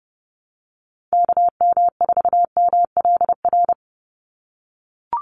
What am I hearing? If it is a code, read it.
Morse code, heard as a keyed tone, about 30 wpm, 700 Hz. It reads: KM4MLR
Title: USA Call Signs 30wpm